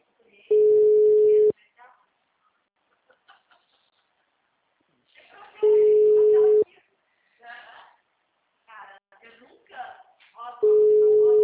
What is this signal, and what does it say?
Ringback tone (Europe, 425 Hz), with speech.
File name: out- -2011- - - .4841